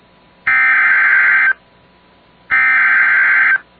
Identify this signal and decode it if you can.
Emergency Alert System (SAME) tones (header bursts), heard as harsh buzzing bursts. It is heard from 0.5 to 3.5 s.